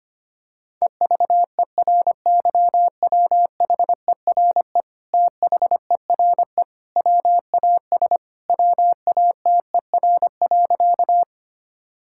Morse code, heard as a keyed tone, about 25 wpm, 700 Hz. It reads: EVERYWHERE THERE WAS WATER.
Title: Sentences from Top 1600 Words 25wpm